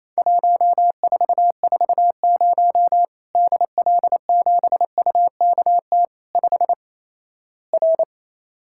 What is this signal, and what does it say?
Morse code, heard as a keyed tone, about 28 wpm, 700 Hz. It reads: 1440 DL7UKT 5 R
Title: Worked All Europe DX Contest - QTC Exchange - 28wpm